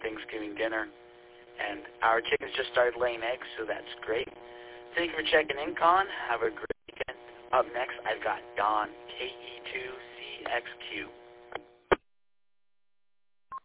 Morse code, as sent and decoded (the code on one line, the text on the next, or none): none